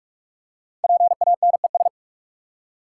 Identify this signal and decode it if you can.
Morse code, heard as a keyed tone, about 44 wpm, 700 Hz. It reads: PANES
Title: Code Groups - 5 Letters - 45wpm